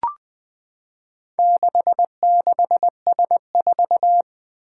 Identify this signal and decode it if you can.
Morse code, heard as a keyed tone, about 20 wpm, 700 Hz. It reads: 66S4
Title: Four Challenging Letter-Number Combinations 20wpm